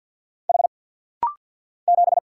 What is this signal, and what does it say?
Morse code, heard as a keyed tone, about 49 wpm, 700 Hz. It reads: H 7
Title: Single Letter-Number Alt - Rapid-Fire 50wpm